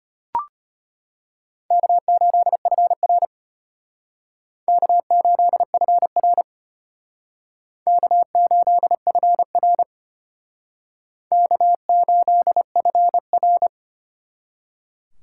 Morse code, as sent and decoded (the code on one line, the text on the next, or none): -.- ---.. ..-. .-. / -.- ---.. ..-. .-. / -.- ---.. ..-. .-. / -.- ---.. ..-. .-.
K8FR K8FR K8FR K8FR